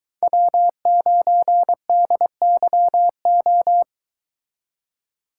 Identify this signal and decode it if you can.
Morse code, heard as a keyed tone, about 22 wpm, 700 Hz. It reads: W9DYO